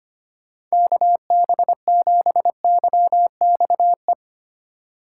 Morse code, heard as a keyed tone, about 25 wpm, 700 Hz. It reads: KB7YXE
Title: USA Call Signs - 6 Characters - 4x Character Spacing - 25wpm